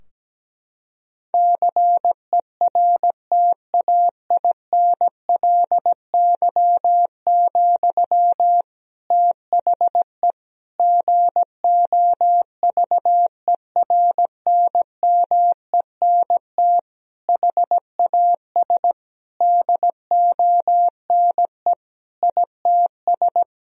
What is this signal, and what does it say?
Morse code, heard as a keyed tone, about 17 wpm, 700 Hz. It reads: CERTAINLY, THE GOVERNMENT HAS DONE ITS
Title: Sentences from Top 1000 Words - Review Part 3 - 17wpm